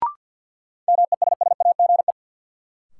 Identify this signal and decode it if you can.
Morse code, heard as a keyed tone, about 53 wpm, 700 Hz. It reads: MESSAGE
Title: Sentences from Top 1100 Words - New Word Drill Alt - 50wpm